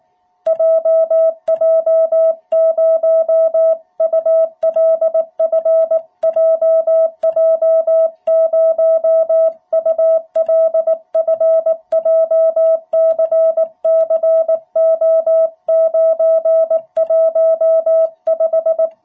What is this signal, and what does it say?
Morse code, heard as a keyed tone, about 18 wpm, 660 Hz. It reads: JJ0ULF JJ0ULF JCCO915
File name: jh8cbh / 2021cw